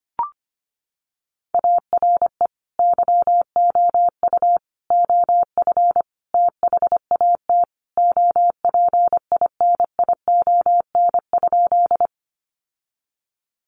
Morse code, heard as a keyed tone, about 25 wpm, 700 Hz. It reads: ARE YOU OF THAT OPINION?
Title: Sentences from Top 1200 Words 25wpm